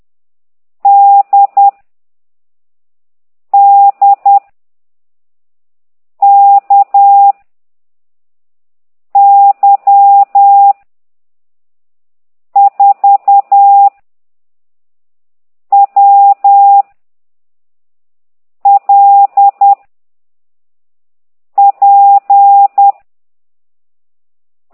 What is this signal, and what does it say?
Morse code, heard as a keyed tone, about 10 wpm, 800 Hz. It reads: D D K Y 4 W L P